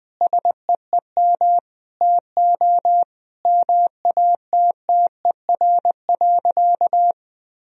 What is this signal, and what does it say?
Morse code, heard as a keyed tone, about 21 wpm, 700 Hz. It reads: SEEM TO MATTER.
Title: Sentences from Top 1000 Words - Review Part 1 - 20wpm